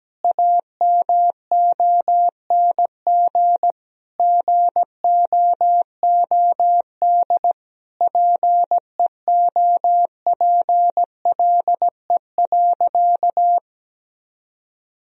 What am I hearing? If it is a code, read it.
Morse code, heard as a keyed tone, about 17 wpm, 700 Hz. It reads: AMONG GOOD PEOPLE.
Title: Sentences from Top 1000 Words - Review Part 3 - 17wpm